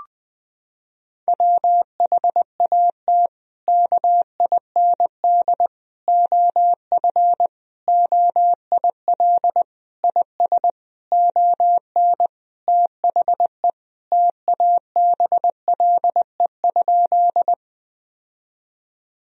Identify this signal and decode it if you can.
Morse code, heard as a keyed tone, about 21 wpm, 700 Hz. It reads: WHAT KIND OF OIL IS ON THE TABLE?
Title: Sentences from Top 1000 Words - Review Part 4 - 20wpm